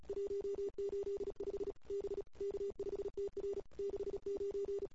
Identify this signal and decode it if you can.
Morse code, heard as a keyed tone, about 32 wpm, 410 Hz. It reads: 185 B K5TR 69
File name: CW / 2008-ss-cw-k5tr-op-n5rz -22:00